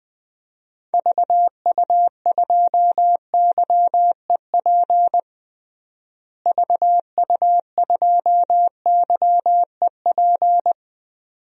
Morse code, heard as a keyed tone, about 20 wpm, 700 Hz. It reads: VU2YEP VU2YEP
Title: DXCC-100 Call Signs - 6 Characters - Repeated 2x - 20wpm